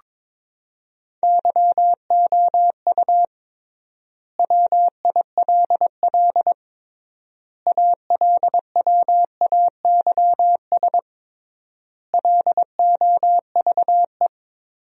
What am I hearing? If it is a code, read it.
Morse code, heard as a keyed tone, about 22 wpm, 700 Hz. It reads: YOU WILL ALWAYS LOVE
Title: 3x Word-Spacing - Sentences from Top 400 Words - 22wpm